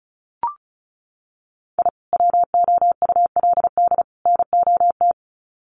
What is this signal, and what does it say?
Morse code, heard as a keyed tone, about 34 wpm, 700 Hz. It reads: I WOULD NOT
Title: Sets of 3 Words 35wpm